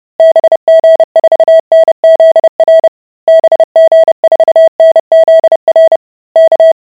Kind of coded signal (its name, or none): Morse code